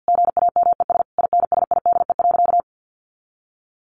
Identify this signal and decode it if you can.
Morse code, heard as a keyed tone, about 50 wpm, 700 Hz. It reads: GAMES INSIDE.
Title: Easy Phrases 50wpm